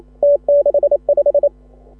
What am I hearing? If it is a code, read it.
Morse code, heard as a keyed tone, about 29 wpm, 560 Hz. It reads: T65
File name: AUDIO2018 / HA6OI CW